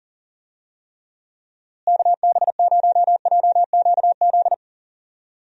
Morse code, heard as a keyed tone, about 40 wpm, 700 Hz. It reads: KB0JQZ
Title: USA Call Signs - 6 Characters - Repeated 3x - 40wpm